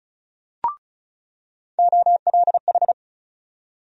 Morse code, heard as a keyed tone, about 34 wpm, 700 Hz. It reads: OLH
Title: USA Call Sign Postfixes - 3 Letters 35wpm